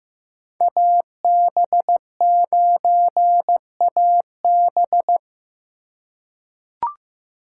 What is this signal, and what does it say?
Morse code, heard as a keyed tone, about 15 wpm, 700 Hz. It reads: AB9AB
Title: USA Call Signs - 5 Characters - 4x Character Spacing - 15wpm